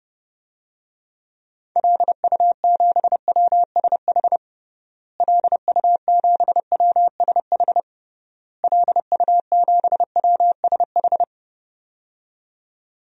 Morse code, heard as a keyed tone, about 30 wpm, 700 Hz. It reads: LU7WSH LU7WSH LU7WSH
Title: DXCC-100 Call Signs - 6 Characters - Repeated 3x - 30wpm